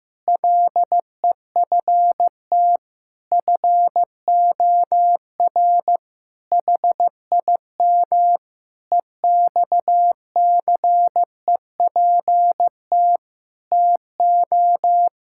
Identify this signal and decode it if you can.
Morse code, heard as a keyed tone, about 15 wpm, 700 Hz. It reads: LEFT FOR HIM EXCEPT TO
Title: Sentences from Top 1600 Words 15wpm